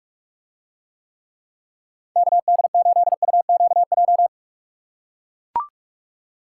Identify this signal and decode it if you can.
Morse code, heard as a keyed tone, about 44 wpm, 700 Hz. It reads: KD8UQJ